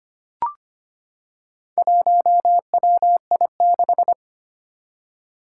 Morse code, heard as a keyed tone, about 25 wpm, 700 Hz. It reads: 1WI6
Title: Four Challenging Letter-Number Combinations 25wpm